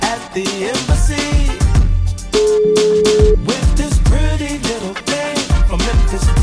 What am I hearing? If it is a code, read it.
Ringback tone (Europe, 425 Hz), with noise.